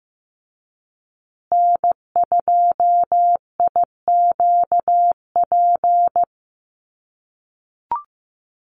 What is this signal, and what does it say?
Morse code, heard as a keyed tone, about 15 wpm, 700 Hz. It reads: N2IQP
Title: USA Call Signs Repeated 3x 15wpm